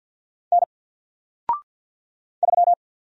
Morse code, heard as a keyed tone, about 50 wpm, 700 Hz. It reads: N 3
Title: Single Character (Letter:Number:Punctuation:Prosign) 50wpm